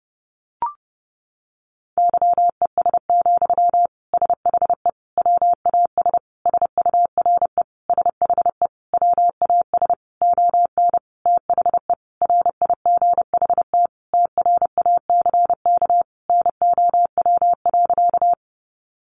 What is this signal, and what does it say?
Morse code, heard as a keyed tone, about 30 wpm, 700 Hz. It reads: YES, SHE WAS SURE SHE WAS ON THE RIGHT TRACK NOW.
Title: Sentences from Top 1100 Words 30wpm